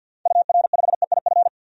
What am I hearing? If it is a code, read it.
Morse code, heard as a keyed tone, about 50 wpm, 700 Hz. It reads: UR5EIF
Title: DXCC-100 Call Signs - 6 Characters - 50wpm